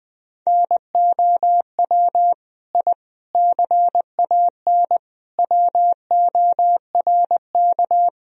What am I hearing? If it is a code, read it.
Morse code, heard as a keyed tone, about 20 wpm, 700 Hz. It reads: NOW I CAN WORK
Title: Sentences from Top 1000 Words - Review Part 1 - 20wpm